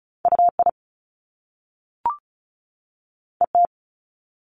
Morse code, heard as a keyed tone, about 36 wpm, 700 Hz. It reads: UI ET
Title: Top Two Letter Combinations in Words 35wpm